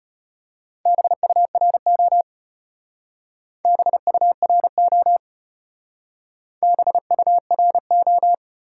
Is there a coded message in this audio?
Yes: Morse code from 0.9 to 8.3 s, heard as a keyed tone, about 32 wpm, 700 Hz. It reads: BURO BURO BURO